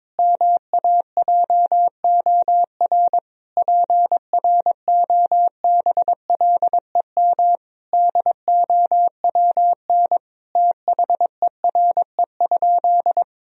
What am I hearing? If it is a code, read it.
Morse code, heard as a keyed tone, about 22 wpm, 700 Hz. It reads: MAJOR PROBLEM DOWN THERE?
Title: Sentences from Top 1000 Words - Review Part 4 - 22wpm